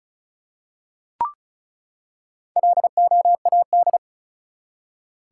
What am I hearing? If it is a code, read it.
Morse code, heard as a keyed tone, about 36 wpm, 700 Hz. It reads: LOAD
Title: Four Letter Words - 35wpm